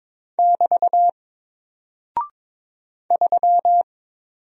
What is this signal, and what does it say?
Morse code, heard as a keyed tone, about 22 wpm, 700 Hz. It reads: <BT> 3
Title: Single Character (Letter:Number:Punctuation:Prosign) Alt 22wpm